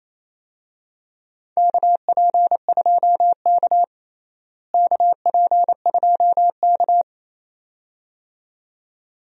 Morse code, heard as a keyed tone, about 27 wpm, 700 Hz. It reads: KP2K KP2K
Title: DXCC-100 Call Signs - 4 Characters - Repeated 2x - 28wpm